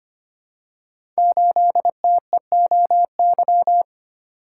Morse code, heard as a keyed tone, about 25 wpm, 700 Hz. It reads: 8TEOY
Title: Code Groups - 5 Letter:Numbers - 25wpm